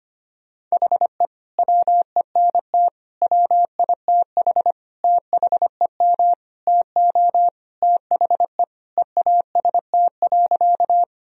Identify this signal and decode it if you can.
Morse code, heard as a keyed tone, about 25 wpm, 700 Hz. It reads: HE WENT WITH THEM TO THE EAST.
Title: Sentences from Top 1000 Words - Review Part 3 - 25wpm